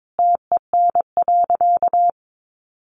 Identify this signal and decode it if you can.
Morse code, heard as a keyed tone, about 22 wpm, 700 Hz. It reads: TEN.
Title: Sentences from Top 700 Words 22wpm